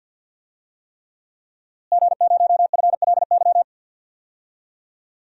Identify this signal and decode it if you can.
Morse code, heard as a keyed tone, about 50 wpm, 700 Hz. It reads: G0FLY
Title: DXCC-100 Call Signs - 5 Characters - Repeated 3x - 50wpm